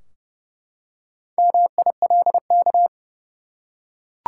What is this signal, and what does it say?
Morse code, heard as a keyed tone, about 32 wpm, 700 Hz. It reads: MILK
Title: Four Letter Words - 30wpm